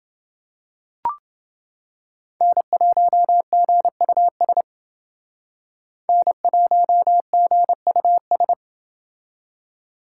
Morse code, heard as a keyed tone, about 29 wpm, 700 Hz. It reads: N1GUS N1GUS